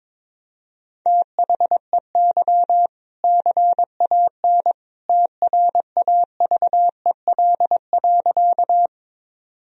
Morse code, heard as a keyed tone, about 22 wpm, 700 Hz. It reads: THEY CAN TRAVEL.
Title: Sentences from Top 1000 Words - Review Part 2 - 22wpm